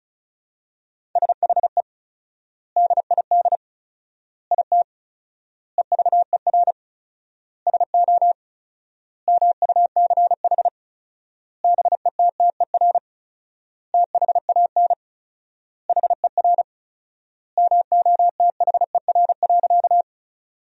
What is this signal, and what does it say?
Morse code, heard as a keyed tone, about 36 wpm, 700 Hz. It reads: SHE DID IT EVER SO MUCH BETTER THAN HER MOTHER.
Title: 4x Word-Spacing - Sentences from Top 400 Words - 35wpm